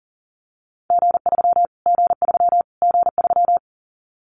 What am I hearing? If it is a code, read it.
Morse code, heard as a keyed tone, about 40 wpm, 700 Hz. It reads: G3 G3 G3
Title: Callsign Prefixes for Top 25 DXCC - Repeated 3x - 40wpm